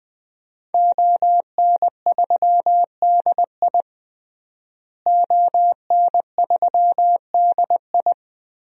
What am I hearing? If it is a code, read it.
Morse code, heard as a keyed tone, about 20 wpm, 700 Hz. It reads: ON3DI ON3DI
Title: DXCC-100 Call Signs - 5 Characters - Repeated 3x - 20wpm